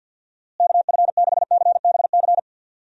Morse code, heard as a keyed tone, about 51 wpm, 700 Hz. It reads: KF6CBC
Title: USA Call Signs - 6 Characters - Repeated 2x - 50wpm